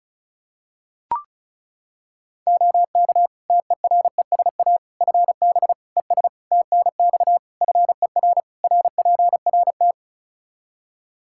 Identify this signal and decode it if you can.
Morse code, heard as a keyed tone, about 36 wpm, 700 Hz. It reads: OK TERESA FB ES TNX FER RPRT
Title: Practice QSO - Exchange 2 - Info - 35wpm